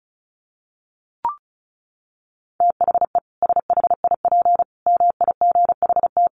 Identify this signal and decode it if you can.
Morse code, heard as a keyed tone, about 34 wpm, 700 Hz. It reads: THE SHIP MIGHT
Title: Sentences from Top 1400 Words 35wpm